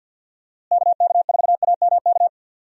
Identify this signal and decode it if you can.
Morse code, heard as a keyed tone, about 48 wpm, 700 Hz. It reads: KK4AMK
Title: USA Call Signs - 6 Characters - Repeated 3x - 50wpm